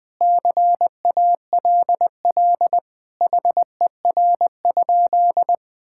Morse code, heard as a keyed tone, about 20 wpm, 700 Hz. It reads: CALL HER?
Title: Sentences with 2-Word Phrases - Spoken Beginning - 20wpm